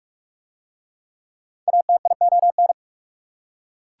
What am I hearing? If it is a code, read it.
Morse code, heard as a keyed tone, about 41 wpm, 690 Hz. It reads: ATION